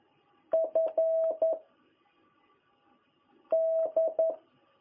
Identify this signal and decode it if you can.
Morse code, heard as a keyed tone, about 11 wpm, 660 Hz. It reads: F D
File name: jh8cbh / 2021cw